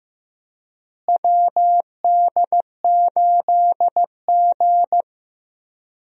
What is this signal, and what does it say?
Morse code, heard as a keyed tone, about 16 wpm, 700 Hz. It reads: WD8G
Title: USA Call Signs - 4 Characters - Repeated 2x - 15wpm